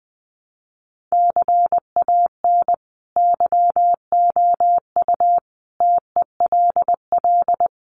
Morse code, heard as a keyed tone, about 20 wpm, 700 Hz. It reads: CAN YOU TELL
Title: Sets of 5 Words Encore2 20wpm